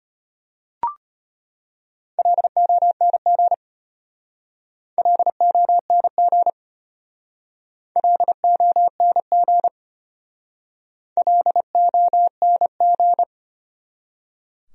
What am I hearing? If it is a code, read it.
Morse code, heard as a keyed tone, about 32 wpm, 700 Hz. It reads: LONG LONG LONG LONG